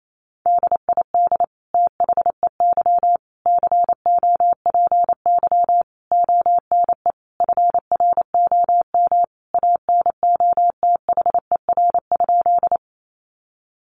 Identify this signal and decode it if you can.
Morse code, heard as a keyed tone, about 28 wpm, 700 Hz. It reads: DID THEY COPY ONE FROM ANOTHER?
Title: Sentences from Top 1400 Words 28wpm